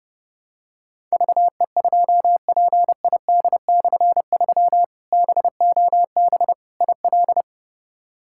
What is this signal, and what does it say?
Morse code, heard as a keyed tone, about 30 wpm, 700 Hz. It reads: VE2PID/3 BOB IL
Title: K1USN SST QSO Practice - 30wpm